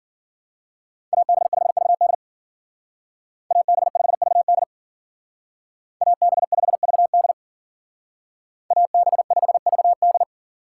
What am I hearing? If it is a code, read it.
Morse code, heard as a keyed tone, about 49 wpm, 700 Hz. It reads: AB5VD AB5VD AB5VD AB5VD
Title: Speed Racing - USA 2x2 Call Signs 40wpm